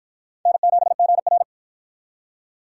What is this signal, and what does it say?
Morse code, heard as a keyed tone, about 53 wpm, 700 Hz. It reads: N7GR